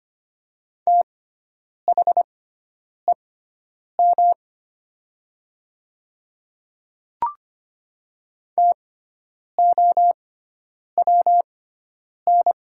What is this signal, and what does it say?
Morse code, heard as a keyed tone, about 25 wpm, spaced as for 9 wpm, 700 Hz. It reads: THEM TOWN
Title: 6x Character Spacing - Top 500 Words - 25wpm